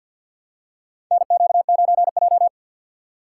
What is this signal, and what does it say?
Morse code, heard as a keyed tone, about 51 wpm, 690 Hz. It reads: NQ9J
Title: USA Call Signs - 6x Character Spacing - 50wpm